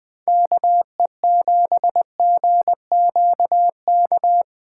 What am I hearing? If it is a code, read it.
Morse code, heard as a keyed tone, about 20 wpm, 700 Hz. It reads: KE7GQK
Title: Speed Racing - USA Call Signs 17wpm